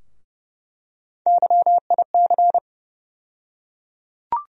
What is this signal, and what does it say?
Morse code, heard as a keyed tone, about 30 wpm, 700 Hz. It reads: YIC
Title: USA Call Sign Postfixes - 3 Letters 30wpm